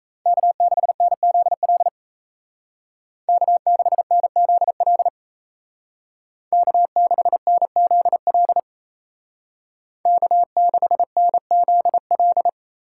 Morse code, heard as a keyed tone, about 34 wpm, 700 Hz. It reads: K6NZL K6NZL K6NZL K6NZL
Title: Speed Racing - USA 1x3 Call Signs 28wpm